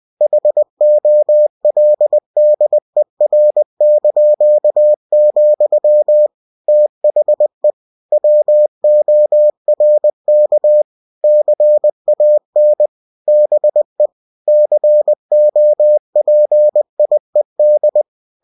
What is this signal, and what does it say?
Morse code, heard as a keyed tone, about 20 wpm, 600 Hz. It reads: HOLDER), THE WORK CAN BE COPIED
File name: cw-books / winnie-the-pooh / winnie-20wpm-page-0115